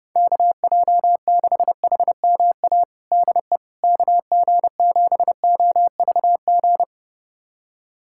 Morse code, heard as a keyed tone, about 30 wpm, 700 Hz. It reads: KJ6HMA DE KG7OVG
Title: USA Callsign DE Callsign - 6 Characters - 30wpm